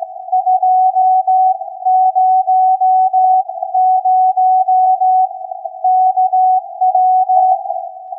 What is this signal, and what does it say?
Morse code, heard as a keyed tone, about 16 wpm, 740 Hz. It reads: 200 KW
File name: iz7slz / VLF / SAQ 16-11-2022